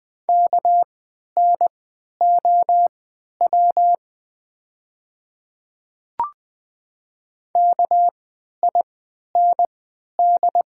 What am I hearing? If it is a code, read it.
Morse code, heard as a keyed tone, about 20 wpm, spaced as for 11 wpm, 700 Hz. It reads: KNOW KIND
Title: 3x Character Spacing - Top 200 Words - 20wpm